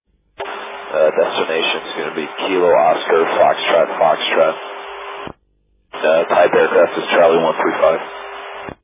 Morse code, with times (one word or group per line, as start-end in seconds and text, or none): none